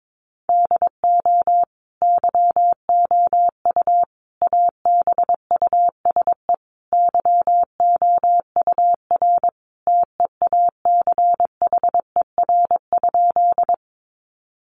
Morse code, heard as a keyed tone, about 22 wpm, 700 Hz. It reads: DO YOU ABUSE YOUR TEACHER?
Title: Sentences from Top 1300 Words 22wpm